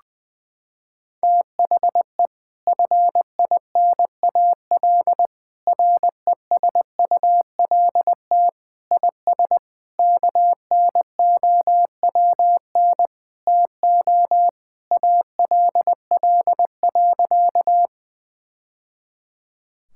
Morse code, heard as a keyed tone, about 20 wpm, 700 Hz. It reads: THE FINAL RESULT IS KNOWN TO ALL.
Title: Sentences from Top 1000 Words - Review Part 5 - 20wpm